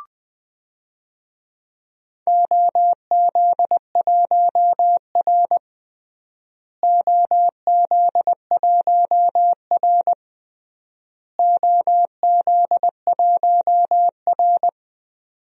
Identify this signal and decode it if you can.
Morse code, heard as a keyed tone, about 21 wpm, 700 Hz. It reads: OZ1R OZ1R OZ1R
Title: DXCC-100 Call Signs - 4 Characters - Repeated 3x - 20wpm